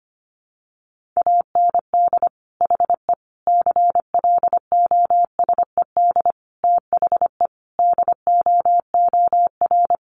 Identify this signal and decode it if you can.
Morse code, heard as a keyed tone, about 25 wpm, 700 Hz. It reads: AND HE CLOSED THE DOOR
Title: Sentences from Top 1500 Words 25wpm